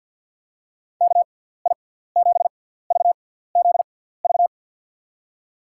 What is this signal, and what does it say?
Morse code, heard as a keyed tone, about 52 wpm, spaced as for 17 wpm, 700 Hz. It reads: KI7VZV
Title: USA Call Signs - 6 Characters - 6x Character Spacing - 50wpm